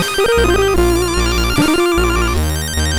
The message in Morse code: none